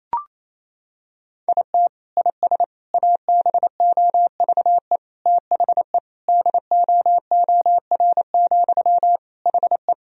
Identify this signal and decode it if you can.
Morse code, heard as a keyed tone, about 28 wpm, 700 Hz. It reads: IT IS ABOVE THE DOOR, HE
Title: Sentences from Top 1000 Words - Review Part 2 - 28wpm